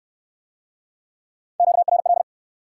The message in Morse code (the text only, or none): QRL